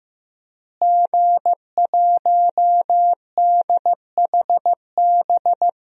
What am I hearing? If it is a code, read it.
Morse code, heard as a keyed tone, about 15 wpm, 700 Hz. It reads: G1DHB
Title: DXCC-100 Call Signs - 5 Characters - Repeated 3x - 15wpm